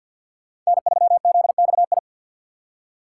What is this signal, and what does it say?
Morse code, heard as a keyed tone, about 47 wpm, 700 Hz. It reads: N3ZXI
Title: USA Call Signs - 5 Characters - 4x Character Spacing - 50wpm